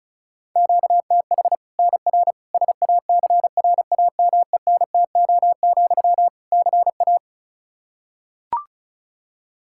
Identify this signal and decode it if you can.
Morse code, heard as a keyed tone, about 36 wpm, 700 Hz. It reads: QTH NR SACRAMENTO, CA
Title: Practice QSO - Exchange 1 - Beginning - 35wpm